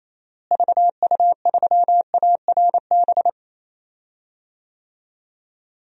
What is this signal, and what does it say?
Morse code, heard as a keyed tone, about 28 wpm, 700 Hz. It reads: VU3ARB
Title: DXCC-100 Call Signs - 6 Characters - Repeated 2x - 28wpm